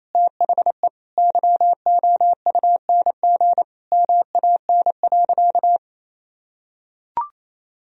Morse code, heard as a keyed tone, about 28 wpm, 700 Hz. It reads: THE YOUNG MAN.
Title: Sentences from Top 1000 Words - Review Part 2 - 28wpm